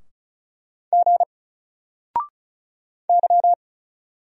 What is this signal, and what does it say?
Morse code, heard as a keyed tone, about 36 wpm, 700 Hz. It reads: G Y
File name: Single-Letter-Number-35wpm